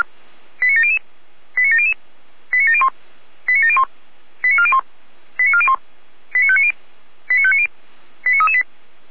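Five-tone selective call (ZVEI), heard as quick runs of five tones, tones of about 70 ms. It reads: 89700 89700 89701 89701 89401 89401 89590 89590 89338
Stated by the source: ZVEI selective call received by SDR.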